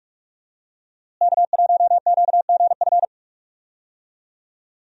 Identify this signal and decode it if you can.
Morse code, heard as a keyed tone, about 46 wpm, 700 Hz. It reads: K1QGF
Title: USA Call Signs - 5 Characters - Repeated 3x - 45wpm